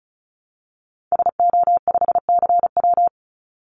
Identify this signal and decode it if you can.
Morse code, heard as a keyed tone, about 34 wpm, 700 Hz. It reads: SO5CW
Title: Call Signs - 35wpm